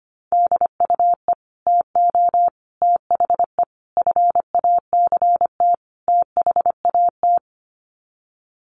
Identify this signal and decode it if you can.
Morse code, heard as a keyed tone, about 25 wpm, 700 Hz. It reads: DUE TO THE FACT THAT
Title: Sets of 5 Words Encore 25wpm